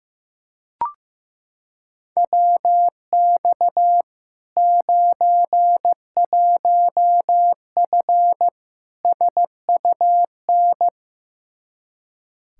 Morse code, heard as a keyed tone, about 16 wpm, 700 Hz. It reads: WX 91F SUN
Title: Practice QSO - Exchange 2 - Info - 15wpm